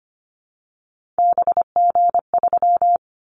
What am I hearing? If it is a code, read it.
Morse code, heard as a keyed tone, about 25 wpm, 700 Hz. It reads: BG3
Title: Callsign Prefixes for Top 100 DXCC - 25wpm